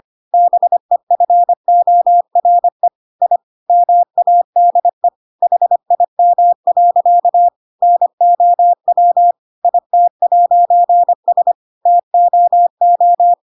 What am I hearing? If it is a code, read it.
Morse code, heard as a keyed tone, about 25 wpm, 700 Hz. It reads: BEFORE I MADE HIM. NOW IT'S TOO
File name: Pinocchio / Pinocchio ch3 25 25 ik1hge